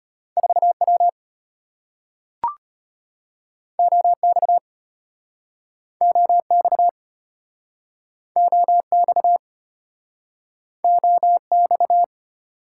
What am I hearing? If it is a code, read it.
Morse code, heard as a keyed tone, about 34 wpm, 700 Hz. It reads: 4W OX OX OX OX